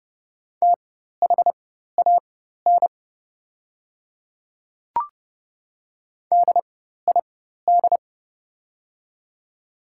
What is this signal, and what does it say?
Morse code, heard as a keyed tone, about 32 wpm, spaced as for 14 wpm, 700 Hz. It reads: THAN DID
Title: 4x Character Spacing - Top 300 Words - 30wpm